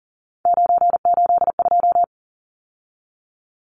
Morse code, heard as a keyed tone, about 40 wpm, 700 Hz. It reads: 982
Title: Numbers - 3-Digit 40wpm